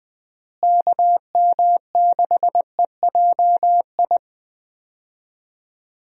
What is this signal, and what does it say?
Morse code, heard as a keyed tone, about 21 wpm, 700 Hz. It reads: KM6EJI
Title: USA Call Signs - 6 Characters - Repeated 3x - 20wpm